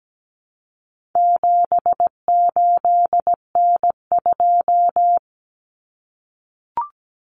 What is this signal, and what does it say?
Morse code, heard as a keyed tone, about 17 wpm, 700 Hz. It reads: 78N2
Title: Common Cut Numbers - 4-Digit 17wpm